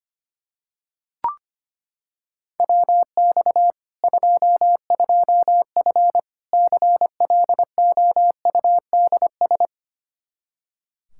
Morse code, heard as a keyed tone, about 25 wpm, 700 Hz. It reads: WX 22F CLOUDS
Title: Practice QSO - Exchange 2 - Info - 25wpm